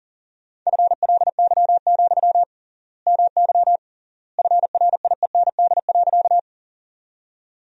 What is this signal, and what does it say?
Morse code, heard as a keyed tone, about 40 wpm, 700 Hz. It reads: FLY, MY FRIEND.
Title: 3x Word-Spacing - Sentences from Top 500 Words - 40wpm